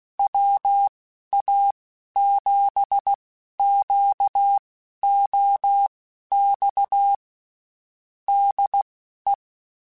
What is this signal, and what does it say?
Morse code, heard as a keyed tone, about 16 wpm, spaced as for 11 wpm, 800 Hz. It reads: WA7QOX DE
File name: (spaced) / QSOs Spaced-mp3s / WA7QOX 2 @15 sp